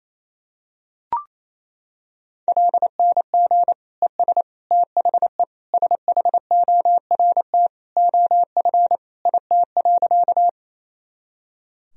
Morse code, heard as a keyed tone, about 28 wpm, 700 Hz. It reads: LNG ES THE SHORT OF IT.
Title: Sentences from Top 400 Words and Abbreviations 28wpm